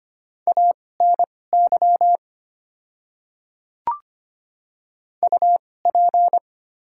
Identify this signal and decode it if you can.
Morse code, heard as a keyed tone, about 25 wpm, spaced as for 18 wpm, 690 Hz. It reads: ANY UP
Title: 2x Character Spacing - Top 300 Words - 25wpm